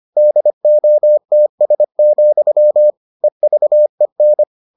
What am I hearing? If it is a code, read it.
Morse code, heard as a keyed tone, about 25 wpm, 600 Hz. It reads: DOTS, EVEN